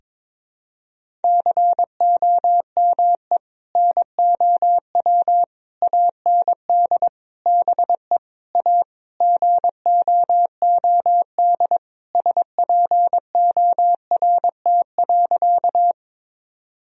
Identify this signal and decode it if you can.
Morse code, heard as a keyed tone, about 22 wpm, 700 Hz. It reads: COME NOW AND BE A GOOD SPORT.
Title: Sentences from Top 1000 Words - Review Part 4 - 22wpm